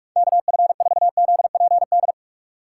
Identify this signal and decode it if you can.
Morse code, heard as a keyed tone, about 46 wpm, 700 Hz. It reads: KF4ZPD
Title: USA Call Signs - 6x Character Spacing - 45wpm